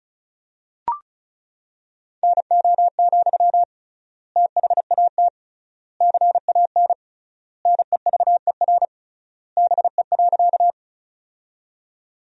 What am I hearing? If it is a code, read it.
Morse code, heard as a keyed tone, about 36 wpm, 700 Hz. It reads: NO, THAT CAN NEVER BE.
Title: 3x Word-Spacing - Sentences from Top 300 Words - 35wpm